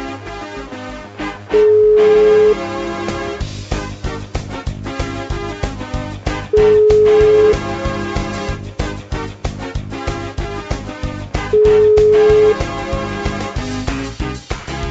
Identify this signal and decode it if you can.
Ringback tone (Europe, 425 Hz).